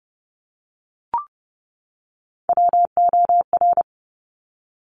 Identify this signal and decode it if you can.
Morse code, heard as a keyed tone, about 30 wpm, 690 Hz. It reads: WOR
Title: Top Three Letter Combinations in Words 30wpm